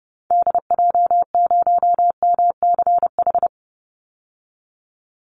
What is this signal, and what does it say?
Morse code, heard as a keyed tone, about 30 wpm, 700 Hz. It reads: DJ0MCH
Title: Call Signs Repeated 2x - 30wpm